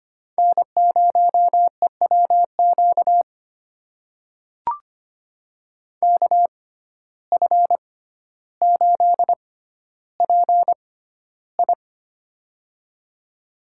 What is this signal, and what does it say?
Morse code, heard as a keyed tone, about 25 wpm, spaced as for 13 wpm, 700 Hz. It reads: N0EWQ KF8PI
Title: USA Call Signs - 5 Characters - 6x Character Spacing - 25wpm